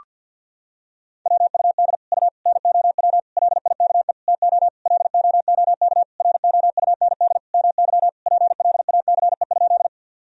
Morse code, heard as a keyed tone, about 50 wpm, 700 Hz. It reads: WUD U NOW LIKE TO LOOK ROUND MY PLACE?
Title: Sentences from Top 200 Words and Abbreviations 50wpm